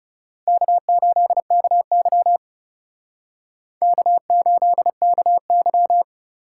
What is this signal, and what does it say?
Morse code, heard as a keyed tone, about 38 wpm, 700 Hz. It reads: K8KY K8KY